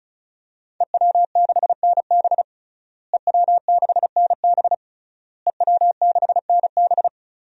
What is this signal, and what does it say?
Morse code, heard as a keyed tone, about 36 wpm, 700 Hz. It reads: EW6NB EW6NB EW6NB
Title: DXCC-100 Call Signs - 5 Characters - Repeated 3x - 35wpm